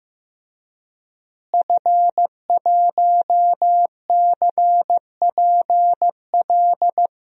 Morse code, heard as a keyed tone, about 15 wpm, 700 Hz. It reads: F1CPL
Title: DXCC-100 Call Signs - 5 Characters - Repeated 2x - 15wpm